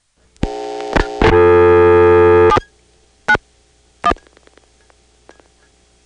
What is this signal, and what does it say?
DTMF, heard as telephone touch tones, tones of about 70 ms. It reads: 791
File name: phonetrips / PrinceWmCoVA5HQ